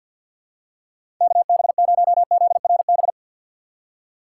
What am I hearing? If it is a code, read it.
Morse code, heard as a keyed tone, about 53 wpm, 700 Hz. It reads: KB0ZRB
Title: USA Call Signs - 6 Characters - 4x Character Spacing - 50wpm